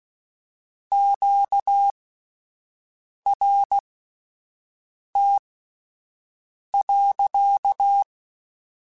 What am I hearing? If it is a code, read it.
Morse code, heard as a keyed tone, about 16 wpm, spaced as for 5 wpm, 800 Hz. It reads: QRT.
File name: Q Calls / Q Calls-mp3s / Q Calls sppc @10